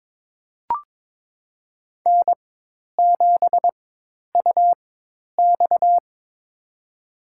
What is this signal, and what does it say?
Morse code, heard as a keyed tone, about 22 wpm, spaced as for 10 wpm, 700 Hz. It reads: N7UX
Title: USA Call Signs - 4 Characters - 4x Character Spacing - 22wpm